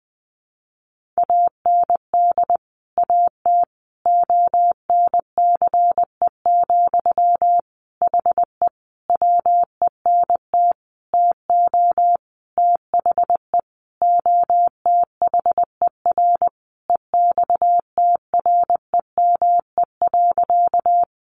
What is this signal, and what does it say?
Morse code, heard as a keyed tone, about 20 wpm, 700 Hz. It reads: AND AT ONCE, HE WENT TO THE OTHER EXTREME.
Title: Sentences from Top 1300 Words 20wpm